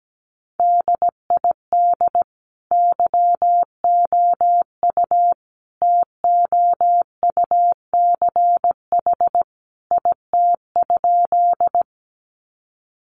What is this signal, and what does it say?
Morse code, heard as a keyed tone, about 17 wpm, 700 Hz. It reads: DID YOU TOUCH IT?
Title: Sentences from Top 1200 Words 17wpm